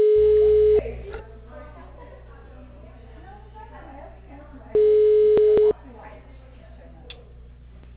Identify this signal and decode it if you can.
Ringback tone (Europe, 425 Hz), with speech.
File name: out- -2003- - - .30439